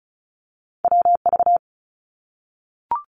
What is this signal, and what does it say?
Morse code, heard as a keyed tone, about 34 wpm, 700 Hz. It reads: WV
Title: US States Repeated 3x 35wpm